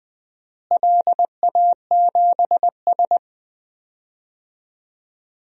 Morse code, heard as a keyed tone, about 20 wpm, 700 Hz. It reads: LA7S
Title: DXCC-100 Call Signs - 4 Characters - Repeated 3x - 20wpm